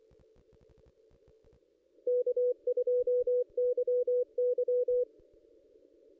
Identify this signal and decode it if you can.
Morse code, heard as a keyed tone, about 24 wpm, 490 Hz. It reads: K2YY
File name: contests / ss-cw-k5tr-op-n5rz -1600